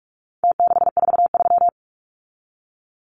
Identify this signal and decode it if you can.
Morse code, heard as a keyed tone, about 44 wpm, 700 Hz. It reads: T643
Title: Numbers - Contest Style - 4-Digit 45wpm